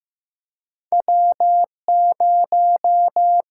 Morse code, heard as a keyed tone, about 15 wpm, 700 Hz. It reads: W0
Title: USA Call Signs - 4 Characters - 6x Character Spacing - 15wpm